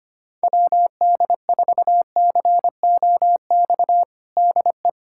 Morse code, heard as a keyed tone, about 25 wpm, 700 Hz. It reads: WD4COX DE
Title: USA Callsign DE Callsign 25wpm